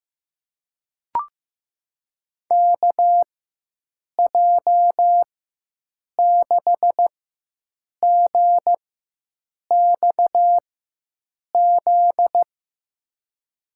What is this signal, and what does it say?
Morse code, heard as a keyed tone, about 15 wpm, spaced as for 7 wpm, 700 Hz. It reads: KJ6GXZ
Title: USA Call Signs - 6 Characters - 4x Character Spacing - 15wpm